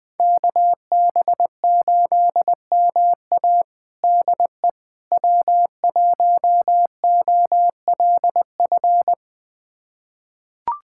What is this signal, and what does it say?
Morse code, heard as a keyed tone, about 20 wpm, 700 Hz. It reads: KB8MA DE W1OLF
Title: USA Callsign DE Callsign - 5 Characters - 20wpm